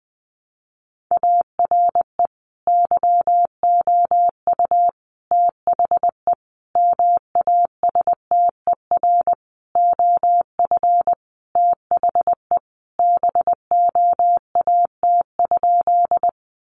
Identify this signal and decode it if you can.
Morse code, heard as a keyed tone, about 20 wpm, 700 Hz. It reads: ARE YOU THE MASTER OF THE BOAT?
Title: Sentences from Top 1300 Words 20wpm